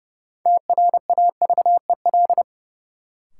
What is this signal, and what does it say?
Morse code, heard as a keyed tone, about 32 wpm, 700 Hz. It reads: TRAVEL
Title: Sentences from Top 400 Words - New Word Drill - 30wpm